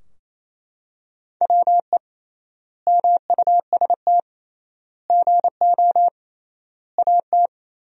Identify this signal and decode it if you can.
Morse code, heard as a keyed tone, about 28 wpm, 700 Hz. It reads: WE MUST GO AT
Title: 3x Word-Spacing - Sentences from Top 300 Words - 28wpm